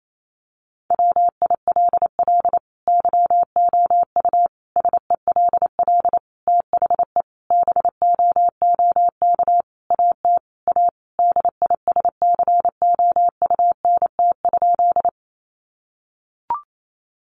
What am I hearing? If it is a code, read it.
Morse code, heard as a keyed tone, about 28 wpm, 700 Hz. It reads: WILL YOU SELL THE BOOK AT A DISCOUNT?
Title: Sentences from Top 1200 Words 28wpm